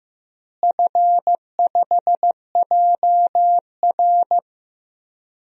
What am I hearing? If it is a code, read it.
Morse code, heard as a keyed tone, about 16 wpm, 700 Hz. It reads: F5JR
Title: DXCC-100 Call Signs - 4 Characters - Repeated 3x - 15wpm